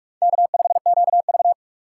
Morse code, heard as a keyed tone, about 46 wpm, 690 Hz. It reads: K5QV
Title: USA Call Signs - 4 Characters - 2x Character Spacing - 45wpm